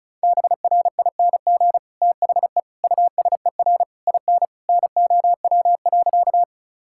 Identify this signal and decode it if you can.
Morse code, heard as a keyed tone, about 36 wpm, 700 Hz. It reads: BRING THE USER IN NOW.
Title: Sentences from Top 1000 Words - Review Part 5 - 35wpm